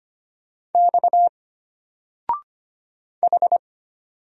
Morse code, heard as a keyed tone, about 25 wpm, 700 Hz. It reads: X H
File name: Single-Letter-Number-25wpm